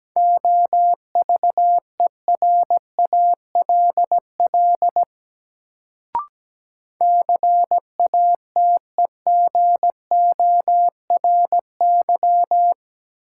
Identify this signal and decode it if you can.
Morse code, heard as a keyed tone, about 17 wpm, 700 Hz. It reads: OVERALL CATEGORY
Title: Sentences from Top 1000 Words - Review Part 7 - New Word Drill - 17wpm